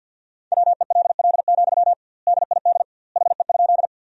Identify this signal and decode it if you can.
Morse code, heard as a keyed tone, about 51 wpm, 700 Hz. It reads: WELL, DID HE?
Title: 2x Word-Spacing - Sentences from Top 200 Words - 50wpm